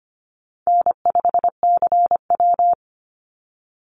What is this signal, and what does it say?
Morse code, heard as a keyed tone, about 25 wpm, 700 Hz. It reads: N5CW
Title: Easy Call Signs - 25wpm